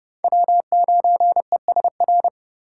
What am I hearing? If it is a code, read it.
Morse code, heard as a keyed tone, about 30 wpm, 700 Hz. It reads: W9ESR
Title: USA Call Signs - 5 Characters - 4x Character Spacing - 30wpm